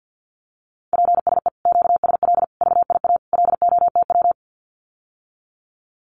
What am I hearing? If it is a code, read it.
Morse code, heard as a keyed tone, about 50 wpm, 700 Hz. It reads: PSE QSL VIA LOTW
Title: Common QSO Phrases Encore 50wpm